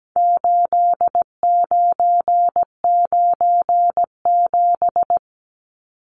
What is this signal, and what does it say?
Morse code, heard as a keyed tone, about 17 wpm, 700 Hz. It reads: 8997
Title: Numbers - 4-Digit 17wpm